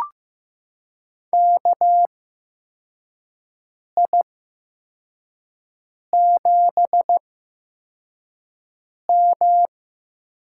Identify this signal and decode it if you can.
Morse code, heard as a keyed tone, about 15 wpm, spaced as for 4 wpm, 700 Hz. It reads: KI7M
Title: USA Call Signs - 8x Character Spacing - 15wpm